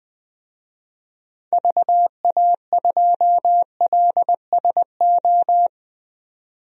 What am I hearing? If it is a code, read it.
Morse code, heard as a keyed tone, about 20 wpm, 700 Hz. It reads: VA2LSO
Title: DXCC-100 Call Signs - 6 Characters - 20wpm